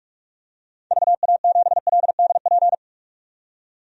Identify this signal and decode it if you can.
Morse code, heard as a keyed tone, about 46 wpm, 700 Hz. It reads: VA7LDP
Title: DXCC-100 Call Signs - 6 Characters - 45wpm